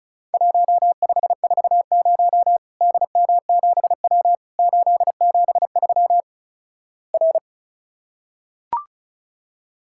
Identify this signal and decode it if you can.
Morse code, heard as a keyed tone, about 36 wpm, 700 Hz. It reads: 1540 DM7W 873 R
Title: Worked All Europe DX Contest - QTC Exchange - 35wpm